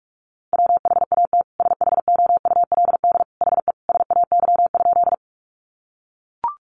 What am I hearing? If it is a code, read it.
Morse code, heard as a keyed tone, about 44 wpm, 700 Hz. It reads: WHAT SHOULD HE SAY?
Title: Sentences from Top 300 Words 45wpm